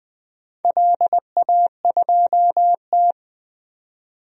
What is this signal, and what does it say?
Morse code, heard as a keyed tone, about 20 wpm, 700 Hz. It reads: LA2T
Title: DXCC-100 Call Signs - 4 Characters - 20wpm